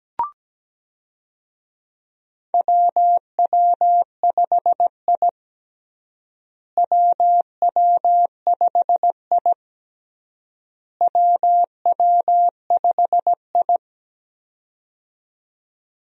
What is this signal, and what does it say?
Morse code, heard as a keyed tone, about 17 wpm, 700 Hz. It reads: WW5I WW5I WW5I
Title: USA Call Signs - 4 Characters - Repeated 3x - 17wpm